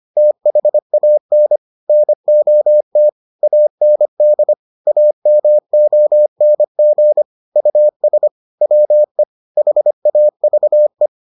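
Morse code, heard as a keyed tone, about 25 wpm, 600 Hz. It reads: THAN NOT AND AMONG US WE HAVE
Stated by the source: Morse edition of the Foundations of Amateur Radio episodes.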